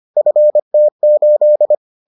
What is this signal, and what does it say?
Morse code, heard as a keyed tone, about 25 wpm, 600 Hz. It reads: FT8